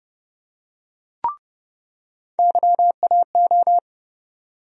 Morse code, heard as a keyed tone, about 30 wpm, 700 Hz. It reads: YAO
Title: USA Call Sign Postfixes - 3 Letters 30wpm